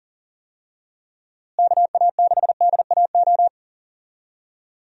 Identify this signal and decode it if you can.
Morse code, heard as a keyed tone, about 40 wpm, 700 Hz. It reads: KA6DAO
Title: USA Call Signs - 6 Characters - Repeated 2x - 40wpm